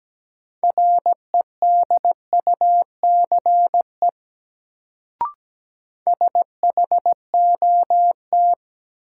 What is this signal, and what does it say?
Morse code, heard as a keyed tone, about 17 wpm, 700 Hz. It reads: REDUCE SHOT
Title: Sentences from Top 1100 Words - New Word Drill - 17wpm